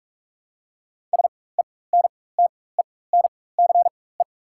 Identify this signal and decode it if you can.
Morse code, heard as a keyed tone, about 47 wpm, spaced as for 21 wpm, 690 Hz. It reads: SENTENCE
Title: 4x Character Spacing - Top 300 Words - 45wpm